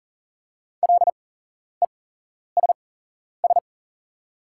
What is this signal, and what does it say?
Morse code, heard as a keyed tone, about 40 wpm, spaced as for 11 wpm, 690 Hz. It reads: LESS
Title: 8x Character Spacing - Top 500 Words - 40wpm